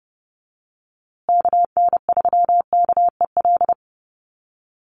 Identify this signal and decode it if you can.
Morse code, heard as a keyed tone, about 30 wpm, 700 Hz. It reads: KN3KEL
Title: USA Call Signs Repeated 3x 30wpm